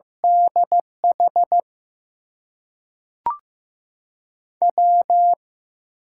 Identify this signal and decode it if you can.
Morse code, heard as a keyed tone, about 15 wpm, 700 Hz. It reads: DH W
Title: USA Call Signs - 5 Characters - 8x Character Spacing - 15wpm